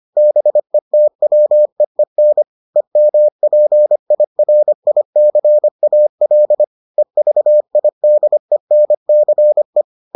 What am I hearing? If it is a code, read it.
Morse code, heard as a keyed tone, about 25 wpm, 600 Hz. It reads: BETWEEN EMPIRICAL EVIDENCE